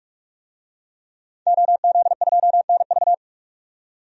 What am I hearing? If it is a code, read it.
Morse code, heard as a keyed tone, about 46 wpm, 700 Hz. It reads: OZ2NV
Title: DXCC-100 Call Signs - 5 Characters - 45wpm